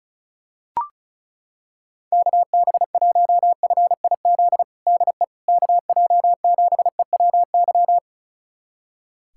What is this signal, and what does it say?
Morse code, heard as a keyed tone, about 36 wpm, 700 Hz. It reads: KB1FIZ DE KJ7EWY
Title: USA Callsign DE Callsign 35wpm